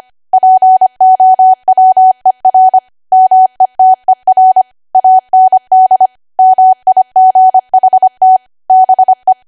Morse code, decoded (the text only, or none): POWER METER AND MIGHT BE